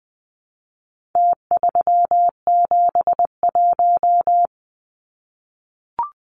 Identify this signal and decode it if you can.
Morse code, heard as a keyed tone, about 20 wpm, 700 Hz. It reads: T371
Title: Numbers - Contest Style - 4-Digit 20wpm